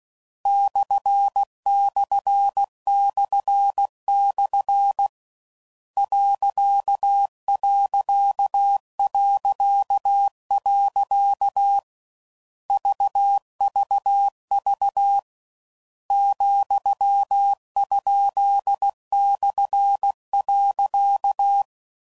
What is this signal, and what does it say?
Morse code, heard as a keyed tone, about 16 wpm, 800 Hz. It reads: //// .... VVV ,?/.
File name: Mixed / Misc / Punct